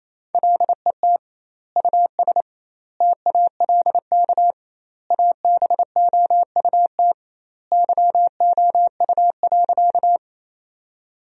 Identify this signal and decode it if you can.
Morse code, heard as a keyed tone, about 28 wpm, 700 Hz. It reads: LET US TALK ABOUT YOU.
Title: 2x Word-Spacing - Sentences from Top 400 Words - 28wpm